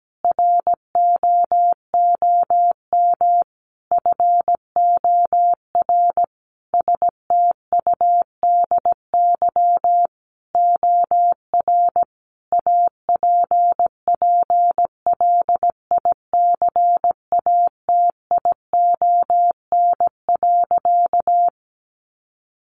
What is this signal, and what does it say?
Morse code, heard as a keyed tone, about 17 wpm, 700 Hz. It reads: ROOM FOR STUDY OR APPLICATION.
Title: Sentences from Top 1200 Words 17wpm